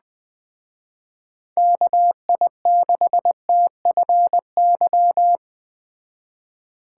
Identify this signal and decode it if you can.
Morse code, heard as a keyed tone, about 20 wpm, 700 Hz. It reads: KI6TFY